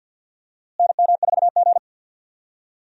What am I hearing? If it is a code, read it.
Morse code, heard as a keyed tone, about 52 wpm, 700 Hz. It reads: NM4G
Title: USA Call Signs - 4 Characters - Repeated 2x - 50wpm